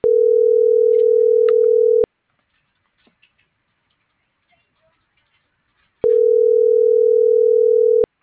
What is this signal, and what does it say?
Ringback tone (North America), 440 + 480 Hz.
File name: CENTRAL SAGOT / out miriam.mercado - .99251